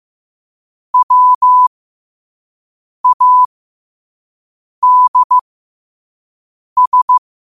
Morse code, heard as a keyed tone, about 15 wpm, spaced as for 5 wpm, 1000 Hz. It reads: WADS
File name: qotdfiles / 5wpm